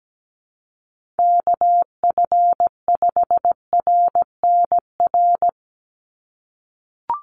Morse code, heard as a keyed tone, about 17 wpm, 700 Hz. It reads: KF5RNR
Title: USA Call Signs 17wpm